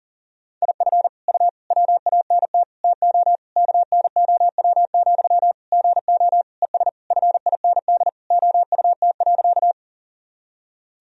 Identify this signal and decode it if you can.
Morse code, heard as a keyed tone, about 37 wpm, 700 Hz. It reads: IF U WANT TO KNOW, GO ES FIND OUT.
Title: Sentences from Top 200 Words and Abbreviations 40wpm